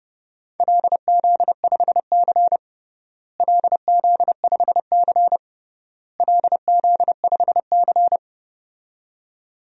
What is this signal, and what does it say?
Morse code, heard as a keyed tone, about 30 wpm, 700 Hz. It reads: LZ5C LZ5C LZ5C
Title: DXCC-100 Call Signs - 4 Characters - Repeated 3x - 30wpm